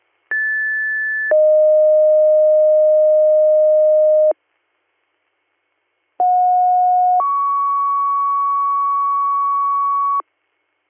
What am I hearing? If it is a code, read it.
Two-tone page, heard as a one-second tone, then a three-second tone, about 1750 then 620 Hz, about 730 then 1120 Hz. It is heard from 0.3 to 10.2 s.